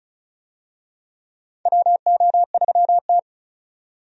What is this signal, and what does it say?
Morse code, heard as a keyed tone, about 36 wpm, 700 Hz. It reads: WO3T
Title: USA Call Signs - 4 Characters - 35wpm